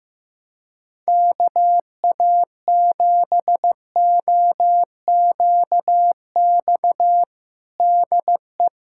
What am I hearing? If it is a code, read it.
Morse code, heard as a keyed tone, about 15 wpm, 700 Hz. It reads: KA7OQX DE
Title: USA Callsign DE Callsign - 6 Characters - 15wpm